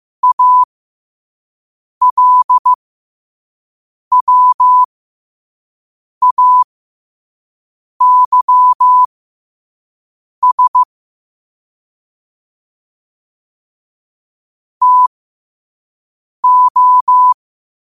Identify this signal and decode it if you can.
Morse code, heard as a keyed tone, about 15 wpm, spaced as for 5 wpm, 1000 Hz. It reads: ALWAYS TO
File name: qotdfiles / 5wpm